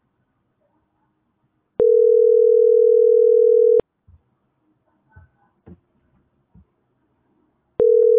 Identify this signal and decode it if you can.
Ringback tone (North America), 440 + 480 Hz.